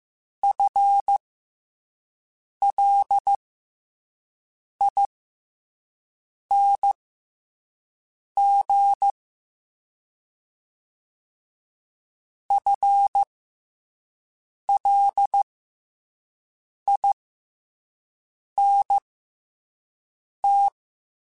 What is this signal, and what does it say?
Morse code, heard as a keyed tone, about 15 wpm, spaced as for 5 wpm, 800 Hz. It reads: FLING FLINT
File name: FL Runs / FL Runs mp3s / FL Runs @5